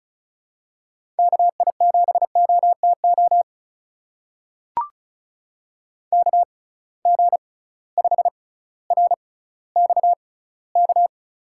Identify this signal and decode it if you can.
Morse code, heard as a keyed tone, about 36 wpm, spaced as for 18 wpm, 700 Hz. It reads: KI7OTO KG5RXK
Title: USA Call Signs - 6 Characters - 6x Character Spacing - 35wpm